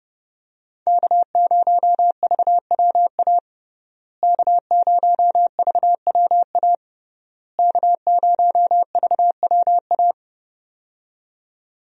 Morse code, heard as a keyed tone, about 30 wpm, 700 Hz. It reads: K0VWA K0VWA K0VWA
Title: USA Call Signs - 5 Characters - Repeated 3x - 30wpm